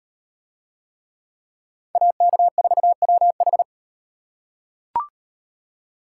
Morse code, heard as a keyed tone, about 36 wpm, 700 Hz. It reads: AK4WH